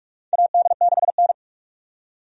Morse code, heard as a keyed tone, about 46 wpm, 700 Hz. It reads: AD6N
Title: USA Call Signs - 4 Characters - 45wpm